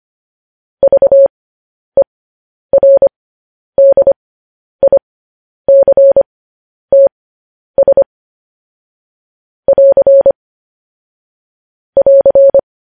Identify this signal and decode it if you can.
Morse code, heard as a keyed tone, about 25 wpm, spaced as for 10 wpm, 550 Hz. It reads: VERDICTS <AR> <AR>
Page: news-headlines-in-morse-code-at-10-wpm